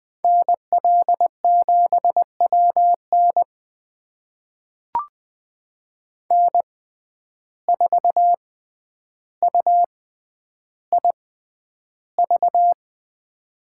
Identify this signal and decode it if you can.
Morse code, heard as a keyed tone, about 20 wpm, spaced as for 10 wpm, 700 Hz. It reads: NL7WN N4UIV
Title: USA Call Signs - 5 Characters - 6x Character Spacing - 20wpm